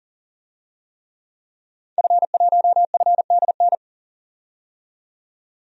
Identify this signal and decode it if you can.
Morse code, heard as a keyed tone, about 40 wpm, 700 Hz. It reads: F1FDN
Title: DXCC-100 Call Signs - 5 Characters - 40wpm